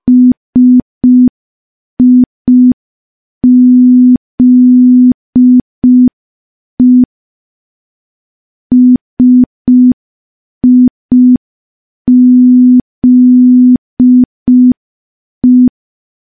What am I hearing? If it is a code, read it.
Morse code, heard as a keyed tone, about 5 wpm, 250 Hz. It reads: SIZE SIZE
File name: morse / morse-training-05WPM